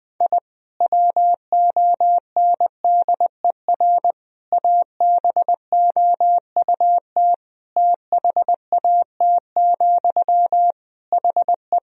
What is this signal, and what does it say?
Morse code, heard as a keyed tone, about 21 wpm, 700 Hz. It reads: I WONDER ABOUT THAT, HE
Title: Sentences from Top 1000 Words - Review Part 3 - 20wpm